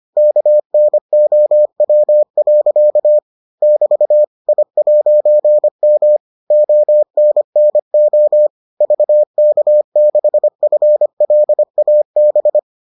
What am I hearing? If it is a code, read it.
Morse code, heard as a keyed tone, about 25 wpm, 600 Hz. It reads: KNOW. <BT> I'M ONNO VK6FLAB